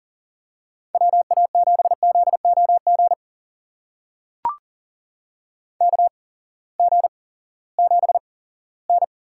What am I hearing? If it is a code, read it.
Morse code, heard as a keyed tone, about 39 wpm, spaced as for 20 wpm, 700 Hz. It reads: WA7ZOG KG7N
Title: USA Call Signs - 6 Characters - 8x Character Spacing - 40wpm